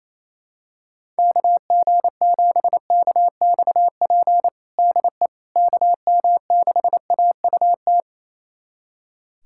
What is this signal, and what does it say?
Morse code, heard as a keyed tone, about 27 wpm, 700 Hz. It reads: KG7KXP DE KM6AUT
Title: USA Callsign DE Callsign - 6 Characters - 28wpm